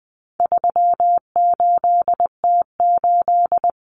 Morse code, heard as a keyed tone, about 20 wpm, 700 Hz. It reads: 38T8
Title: Common Cut Numbers - 4-Digit 20wpm